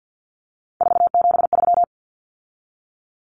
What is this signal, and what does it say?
Morse code, heard as a keyed tone, about 48 wpm, 700 Hz. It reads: 473
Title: Numbers - 3-Digit 50wpm